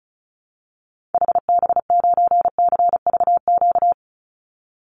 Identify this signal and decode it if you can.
Morse code, heard as a keyed tone, about 34 wpm, 700 Hz. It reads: HB9CVQ
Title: Call Signs Repeated 3x - 35wpm